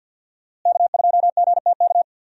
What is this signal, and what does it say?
Morse code, heard as a keyed tone, about 46 wpm, 700 Hz. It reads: K2GTK
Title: USA Call Signs - 5 Characters - Repeated 3x - 50wpm